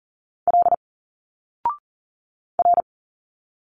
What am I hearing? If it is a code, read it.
Morse code, heard as a keyed tone, about 40 wpm, 700 Hz. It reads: L R
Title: Single Letters 40wpm